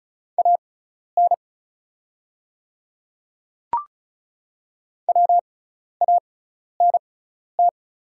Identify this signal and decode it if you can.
Morse code, heard as a keyed tone, about 34 wpm, spaced as for 12 wpm, 700 Hz. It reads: AN WANT
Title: 6x Character Spacing - Top 200 Words - 35wpm